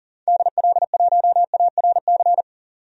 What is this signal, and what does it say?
Morse code, heard as a keyed tone, about 39 wpm, 700 Hz. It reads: DL1ARC
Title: DXCC-100 Call Signs - 6 Characters - 40wpm